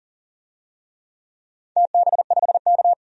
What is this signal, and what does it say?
Morse code, heard as a keyed tone, about 40 wpm, 700 Hz. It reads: TB5K
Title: DXCC-100 Call Signs - 4 Characters - 40wpm